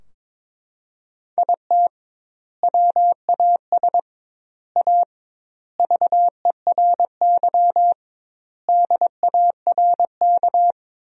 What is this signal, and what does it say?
Morse code, heard as a keyed tone, about 22 wpm, 700 Hz. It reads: IT WAS A VERY DARK
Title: 2x Word-Spacing - Sentences from Top 500 Words - 22wpm